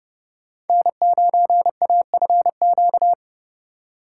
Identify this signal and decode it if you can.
Morse code, heard as a keyed tone, about 39 wpm, 700 Hz. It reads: N9AFQ